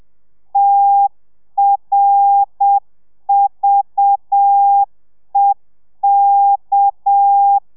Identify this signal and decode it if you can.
Morse code, heard as a keyed tone, about 7 wpm, 800 Hz. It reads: TRVEK